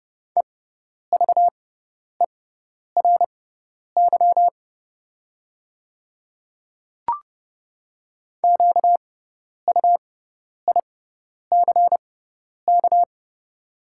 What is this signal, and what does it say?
Morse code, heard as a keyed tone, about 30 wpm, spaced as for 10 wpm, 690 Hz. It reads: EVERY QUICK
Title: 6x Character Spacing - Top 500 Words - 30wpm